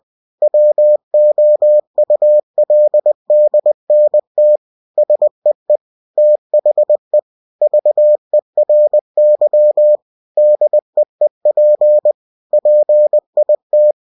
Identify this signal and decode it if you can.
Morse code, heard as a keyed tone, about 20 wpm, 600 Hz. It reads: WOULDNT SEE THE VERY DEEP PIT
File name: cw-books / winnie-the-pooh / winnie-20wpm-page-0035